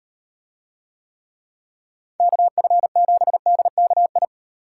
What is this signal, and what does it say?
Morse code, heard as a keyed tone, about 36 wpm, 700 Hz. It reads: KF7DKI